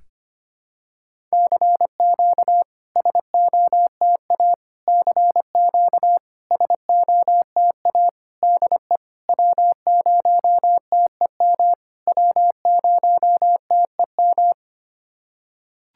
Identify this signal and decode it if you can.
Morse code, heard as a keyed tone, about 25 wpm, 700 Hz. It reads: CQ SOTA CQ SOTA DE W0TEM W0TEM
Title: SOTA QSO Practice 25wpm